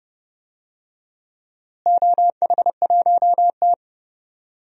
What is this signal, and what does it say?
Morse code, heard as a keyed tone, about 30 wpm, 700 Hz. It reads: OH1T
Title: DXCC-100 Call Signs - 4 Characters - Repeated 3x - 30wpm